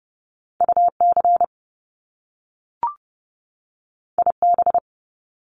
Morse code, heard as a keyed tone, about 30 wpm, 700 Hz. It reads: UC IB
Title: Top Two Letter Combinations in Words 30wpm